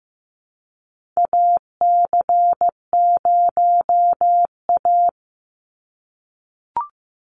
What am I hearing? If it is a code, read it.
Morse code, heard as a keyed tone, about 15 wpm, 700 Hz. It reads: AC0A
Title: NA SOTA Chaser Callsigns 15wpm